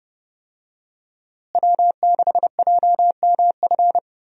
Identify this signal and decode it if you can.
Morse code, heard as a keyed tone, about 30 wpm, 700 Hz. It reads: W6JMF
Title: USA Call Signs - 5 Characters - Repeated 3x - 30wpm